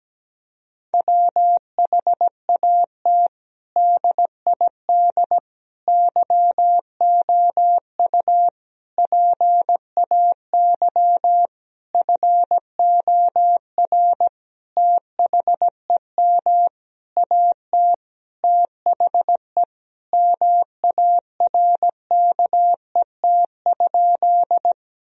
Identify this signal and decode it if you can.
Morse code, heard as a keyed tone, about 17 wpm, 700 Hz. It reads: WHAT DID YOU PAY FOR THEM AT THE MARKET?
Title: Sentences from Top 1000 Words - Review Part 3 - 17wpm